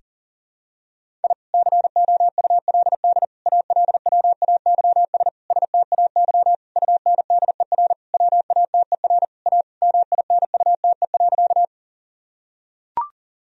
Morse code, heard as a keyed tone, about 40 wpm, 700 Hz. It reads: I COULD ALWAYS STAY UNDER WATER A MINUTE.
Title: Sentences from Top 1000 Words - Review Part 1 - 40wpm